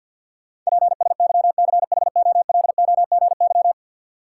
Morse code, heard as a keyed tone, about 48 wpm, 700 Hz. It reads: PSYCHOLOGY
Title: Sentences from Top 1400 Words - New Word Drill Alt - 50wpm